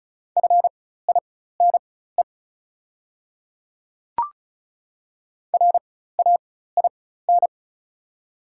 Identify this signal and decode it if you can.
Morse code, heard as a keyed tone, about 36 wpm, spaced as for 16 wpm, 700 Hz. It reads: FINE RAIN
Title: 4x Character Spacing - Top 500 Words - 35wpm